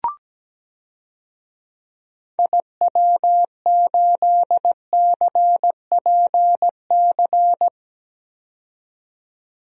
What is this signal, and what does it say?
Morse code, heard as a keyed tone, about 17 wpm, 700 Hz. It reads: IW8CPC
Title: DXCC-100 Call Signs - 6 Characters - 17wpm